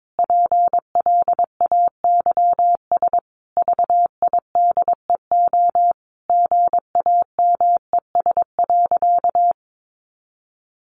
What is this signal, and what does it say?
Morse code, heard as a keyed tone, about 22 wpm, 700 Hz. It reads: PLAYS VIDEO GAMES.
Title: Additional Easy Phrases 22wpm